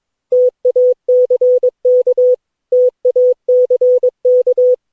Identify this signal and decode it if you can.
Morse code, heard as a keyed tone, about 21 wpm, 500 Hz. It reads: TACK TACK
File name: cw / Session9 cw training